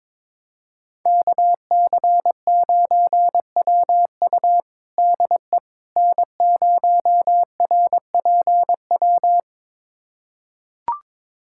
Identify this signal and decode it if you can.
Morse code, heard as a keyed tone, about 22 wpm, 700 Hz. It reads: KC9WU DE N0RPW
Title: USA Callsign DE Callsign - 5 Characters - 22wpm